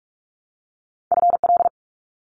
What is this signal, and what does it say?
Morse code, heard as a keyed tone, about 44 wpm, 700 Hz. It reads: FL
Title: US States Repeated 3x 45wpm